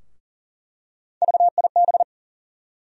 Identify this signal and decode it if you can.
Morse code, heard as a keyed tone, about 39 wpm, 700 Hz. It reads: VIB
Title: USA Call Sign Postfixes - 3 Letters 40wpm